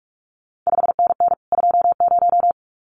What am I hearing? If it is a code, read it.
Morse code, heard as a keyed tone, about 46 wpm, 700 Hz. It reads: 5NN 20
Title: CQ WW DX Contest Exchange Encore - 45wpm